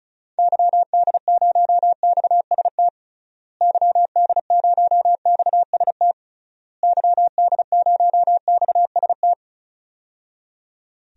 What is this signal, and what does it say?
Morse code, heard as a keyed tone, about 36 wpm, 700 Hz. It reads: YD0XST YD0XST YD0XST
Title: DXCC-100 Call Signs - 6 Characters - Repeated 3x - 35wpm